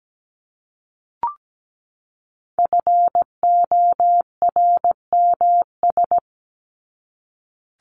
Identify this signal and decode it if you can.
Morse code, heard as a keyed tone, about 17 wpm, 700 Hz. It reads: FORMS
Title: ES:S Suffix - Top 100 Words with Base Words 17wpm